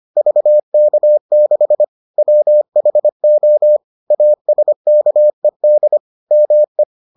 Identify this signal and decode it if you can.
Morse code, heard as a keyed tone, about 25 wpm, 600 Hz. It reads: VK6 WHO ASKED ME